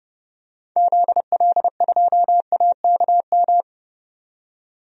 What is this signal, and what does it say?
Morse code, heard as a keyed tone, about 31 wpm, 690 Hz. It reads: ZL2AKM
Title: DXCC-100 Call Signs - 6 Characters - 30wpm